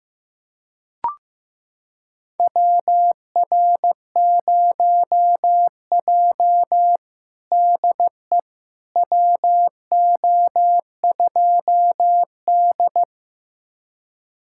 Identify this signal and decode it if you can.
Morse code, heard as a keyed tone, about 15 wpm, 700 Hz. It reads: WR0J DE WO2D
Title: USA Callsign DE Callsign - 4 Characters - 15wpm